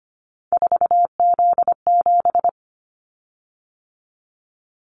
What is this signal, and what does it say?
Morse code, heard as a keyed tone, about 25 wpm, 700 Hz. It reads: 4Z7
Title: Callsign Prefixes for Top 50 DXCC - 25wpm